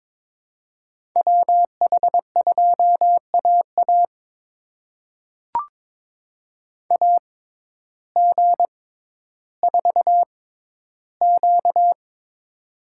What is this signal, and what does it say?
Morse code, heard as a keyed tone, about 22 wpm, spaced as for 12 wpm, 700 Hz. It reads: WH2AA AG4Q
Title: USA Call Signs - 5 Characters - 6x Character Spacing - 22wpm